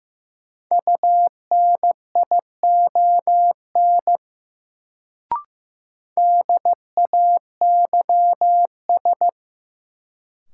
Morse code, heard as a keyed tone, about 15 wpm, 700 Hz. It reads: UNION DAYS
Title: Sets of 5 Words Encore - New Word Drill - 15wpm